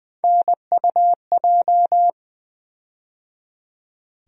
Morse code, heard as a keyed tone, about 20 wpm, 700 Hz. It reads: NUJ
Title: USA Call Sign Postfixes - 3 Letters 20wpm